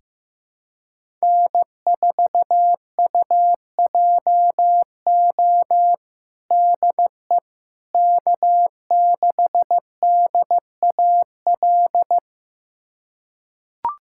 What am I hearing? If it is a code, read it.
Morse code, heard as a keyed tone, about 15 wpm, 700 Hz. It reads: N4UJO DE K6DAL
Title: USA Callsign DE Callsign - 5 Characters - 15wpm